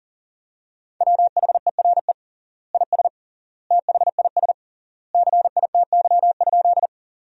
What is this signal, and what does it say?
Morse code, heard as a keyed tone, about 40 wpm, 700 Hz. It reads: WHERE IS THIS CITY?
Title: 3x Word-Spacing - Sentences from Top 300 Words - 40wpm